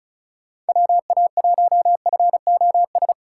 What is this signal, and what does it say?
Morse code, heard as a keyed tone, about 47 wpm, spaced as for 41 wpm, 700 Hz. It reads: WA1FOS
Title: Speed Racing - USA 2x3 Call Signs 35wpm